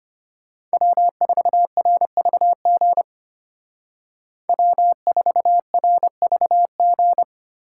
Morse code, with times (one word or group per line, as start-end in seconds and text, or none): 0.7-3.0 W4RVG
4.5-7.2 W4RVG